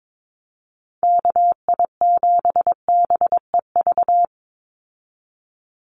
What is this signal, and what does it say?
Morse code, heard as a keyed tone, about 22 wpm, 700 Hz. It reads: KI7BEV
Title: USA Call Signs 22wpm